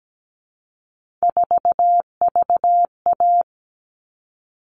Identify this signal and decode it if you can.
Morse code, heard as a keyed tone, about 17 wpm, 700 Hz. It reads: 4VA
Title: All Cut Numbers - 3-Digit 17wpm